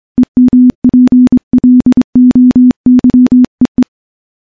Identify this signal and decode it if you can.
Morse code, heard as a keyed tone, about 24 wpm, 250 Hz. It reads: EMPLOYEE
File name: morse-training-25WPM